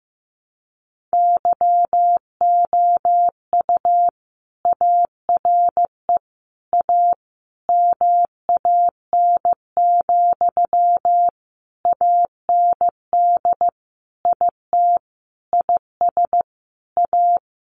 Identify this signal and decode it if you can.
Morse code, heard as a keyed tone, about 15 wpm, 700 Hz. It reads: YOU ARE A MAN, AND IT IS A
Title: Sentences from Top 200 Words 15wpm